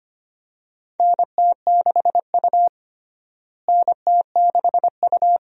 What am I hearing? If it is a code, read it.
Morse code, heard as a keyed tone, about 25 wpm, 700 Hz. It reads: NT6U NT6U
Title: USA Call Signs - 4 Characters - Repeated 2x - 25wpm